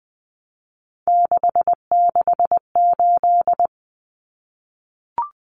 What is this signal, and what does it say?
Morse code, heard as a keyed tone, about 20 wpm, 700 Hz. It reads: 668
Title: Numbers - 3-Digit 20wpm